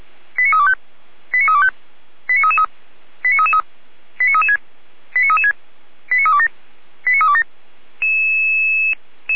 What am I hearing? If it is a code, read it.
Five-tone selective call (ZVEI), heard as quick runs of five tones, tones of about 70 ms. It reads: 89325 89325 89333 89336 89336 89327 89327